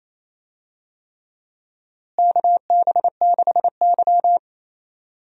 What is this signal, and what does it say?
Morse code, heard as a keyed tone, about 28 wpm, 700 Hz. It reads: KB6Y
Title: USA Call Signs - 4 Characters - Repeated 3x - 28wpm